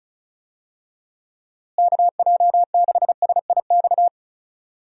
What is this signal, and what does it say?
Morse code, heard as a keyed tone, about 34 wpm, 700 Hz. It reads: KJ6SIX
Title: USA Call Signs - 6 Characters - Repeated 2x - 35wpm